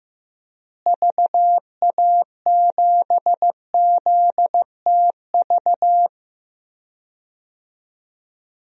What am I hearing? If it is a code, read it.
Morse code, heard as a keyed tone, about 15 wpm, 700 Hz. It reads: VA7ZTV
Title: DXCC-100 Call Signs - 6 Characters - Repeated 2x - 15wpm